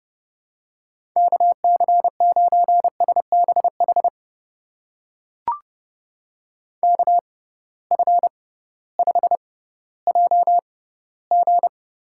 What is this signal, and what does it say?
Morse code, heard as a keyed tone, about 30 wpm, spaced as for 16 wpm, 700 Hz. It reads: KC9SBH KF5JG
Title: USA Call Signs - 6 Characters - 6x Character Spacing - 30wpm